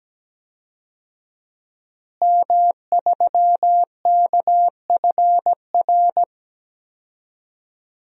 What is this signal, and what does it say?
Morse code, heard as a keyed tone, about 17 wpm, 700 Hz. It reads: M3KFR
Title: DXCC-100 Call Signs - 5 Characters - Repeated 2x - 17wpm